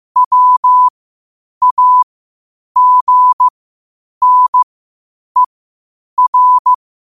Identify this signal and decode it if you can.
Morse code, heard as a keyed tone, about 15 wpm, spaced as for 8 wpm, 1000 Hz. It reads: WAGNER
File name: qotdfiles / 8wpm